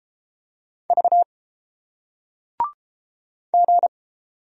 Morse code, heard as a keyed tone, about 30 wpm, 700 Hz. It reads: V G